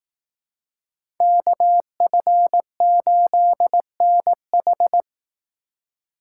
Morse code, heard as a keyed tone, about 19 wpm, 700 Hz. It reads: KF8NH
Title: Speed Racing - USA Call Signs 15wpm